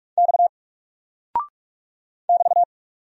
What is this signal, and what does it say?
Morse code, heard as a keyed tone, about 44 wpm, 700 Hz. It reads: X <BT>
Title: Single Character (Letter:Number:Punctuation:Prosign) 45wpm